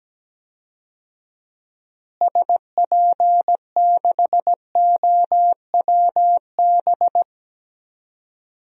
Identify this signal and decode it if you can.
Morse code, heard as a keyed tone, about 17 wpm, 700 Hz. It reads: SP6OWB
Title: DXCC-100 Call Signs - 6 Characters - Repeated 2x - 17wpm